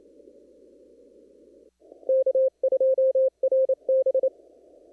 Morse code, heard as a keyed tone, about 28 wpm, 530 Hz. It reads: K2RB
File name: CW / kt5j wpx cw-op-k5pi